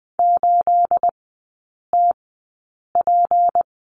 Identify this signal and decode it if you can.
Morse code, heard as a keyed tone, about 20 wpm, spaced as for 8 wpm, 690 Hz. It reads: 8TP
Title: 212 Lesson 34e - Number 8 - Three Characters 20wpm